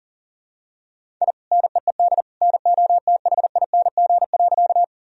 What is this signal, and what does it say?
Morse code, heard as a keyed tone, about 40 wpm, 700 Hz. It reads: I NEED NOTHING.
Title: Sentences from Top 1000 Words - Review Part 1 - 40wpm